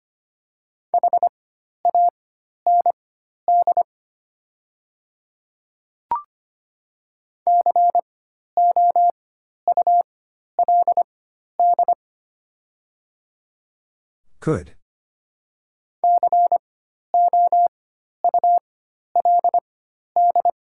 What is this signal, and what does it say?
Morse code, heard as a keyed tone, about 25 wpm, spaced as for 12 wpm, 700 Hz. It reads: HAND COULD COULD
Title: 4x Character Spacing - Top 200 Words - 25wpm